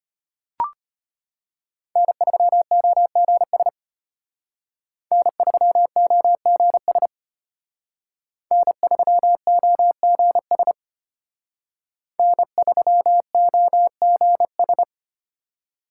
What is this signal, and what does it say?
Morse code, heard as a keyed tone, about 30 wpm, 700 Hz. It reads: N3OGS N3OGS N3OGS N3OGS
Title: Speed Racing - USA 1x3 Call Signs 25wpm